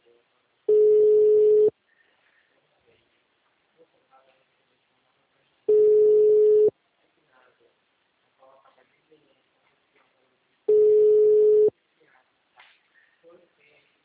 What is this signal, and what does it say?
Ringback tone (Europe, 425 Hz), with speech.
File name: out- -2010- - - .517